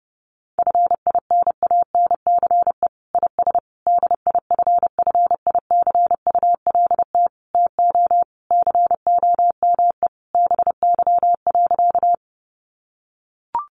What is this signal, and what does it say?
Morse code, heard as a keyed tone, about 30 wpm, 700 Hz. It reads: FINANCE IS DIFFICULT TO COME BY.
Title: Sentences from Top 1500 Words 30wpm